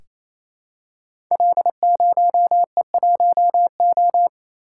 Morse code, heard as a keyed tone, about 29 wpm, 700 Hz. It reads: L0E1O
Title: Code Groups - 5 Letter:Numbers - 28wpm